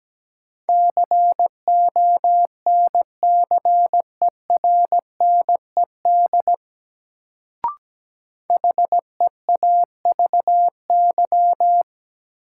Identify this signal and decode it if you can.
Morse code, heard as a keyed tone, about 17 wpm, 700 Hz. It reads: CONCERNED HEAVY
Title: Sentences from Top 1000 Words - Review Part 6 - New Word Drill - 17wpm